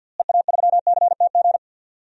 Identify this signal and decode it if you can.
Morse code, heard as a keyed tone, about 52 wpm, 700 Hz. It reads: EA3CTG
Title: DXCC-100 Call Signs - 6 Characters - Repeated 3x - 50wpm